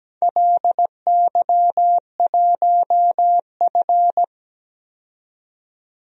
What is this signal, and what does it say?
Morse code, heard as a keyed tone, about 17 wpm, 700 Hz. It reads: LY1F
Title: DXCC-100 Call Signs - 4 Characters - 17wpm